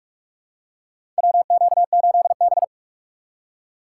Morse code, heard as a keyed tone, about 46 wpm, 700 Hz. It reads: WQ8B
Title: USA Call Signs - 4 Characters - 8x Character Spacing - 45wpm